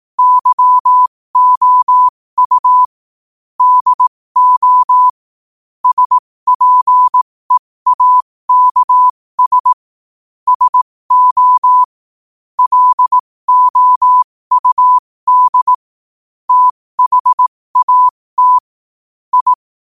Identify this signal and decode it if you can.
Morse code, heard as a keyed tone, about 18 wpm, spaced as for 15 wpm, 1000 Hz. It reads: YOU DO SPEAKS SO LOUD THAT I
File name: qotdfiles / 15wpm